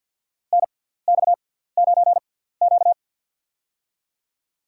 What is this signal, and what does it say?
Morse code, heard as a keyed tone, about 52 wpm, spaced as for 17 wpm, 700 Hz. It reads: NX9Q
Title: USA Call Signs - 6x Character Spacing - 50wpm